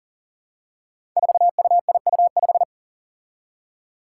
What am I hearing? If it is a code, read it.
Morse code, heard as a keyed tone, about 40 wpm, 700 Hz. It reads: 4UIU5
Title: Five Challenging Letter-Number Combinations 40wpm